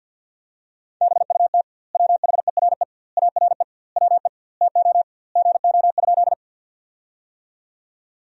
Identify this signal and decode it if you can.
Morse code, heard as a keyed tone, about 50 wpm, 700 Hz. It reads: BUT WHERE ARE WE TO GO?
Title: 2x Word-Spacing - Sentences from Top 200 Words - 50wpm